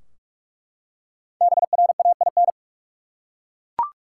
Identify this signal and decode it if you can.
Morse code, heard as a keyed tone, about 42 wpm, 700 Hz. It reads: BRAIN
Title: Five Letter Words - 45wpm